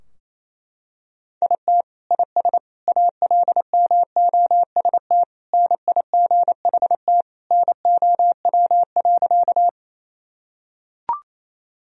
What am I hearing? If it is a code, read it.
Morse code, heard as a keyed tone, about 28 wpm, 700 Hz. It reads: IT IS ALMOST NIGHT NOW.
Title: Sentences from Top 1000 Words - Review Part 4 - 28wpm